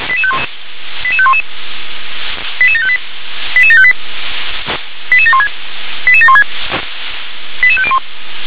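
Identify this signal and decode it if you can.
Five-tone selective call (ZVEI), heard as quick runs of five tones, tones of about 70 ms. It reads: 88511 88758 88758 88716 88716 88501
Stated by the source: ZVEI five-tone selective call received via SDR.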